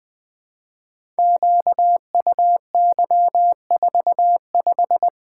Morse code, heard as a keyed tone, about 20 wpm, 700 Hz. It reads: QUY45
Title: Code Groups - 5 Letter:Numbers - 20wpm